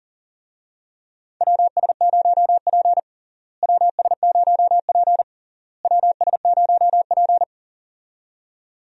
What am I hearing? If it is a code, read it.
Morse code, heard as a keyed tone, about 37 wpm, 700 Hz. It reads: WS0P WS0P WS0P
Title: USA Call Signs - 4 Characters - Repeated 3x - 40wpm